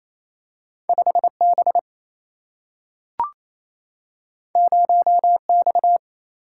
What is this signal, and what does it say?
Morse code, heard as a keyed tone, about 28 wpm, 700 Hz. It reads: 5 B 0 X
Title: Single-Letter Single-Number 28wpm